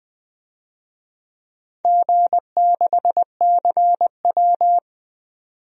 Morse code, heard as a keyed tone, about 20 wpm, 700 Hz. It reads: G6CW
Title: DXCC-100 Call Signs - 4 Characters - 20wpm